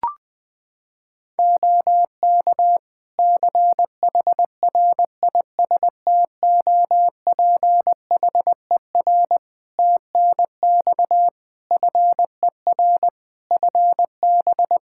Morse code, heard as a keyed tone, about 21 wpm, 700 Hz. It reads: OK CHRISTOPHER TNX FER FB
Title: Practice QSO - Exchange 4 - Ending - 20wpm